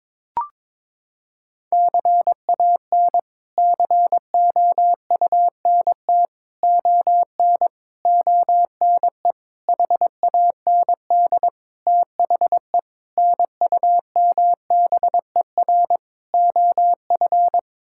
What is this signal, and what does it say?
Morse code, heard as a keyed tone, about 22 wpm, 700 Hz. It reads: CAN COUNT ON ONE HAND THE NUMBER OF
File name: practice / 22wpm / Sets of 8 Words - 22wpm